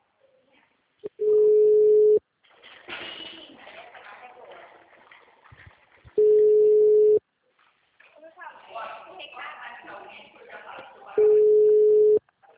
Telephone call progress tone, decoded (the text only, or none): ringback tone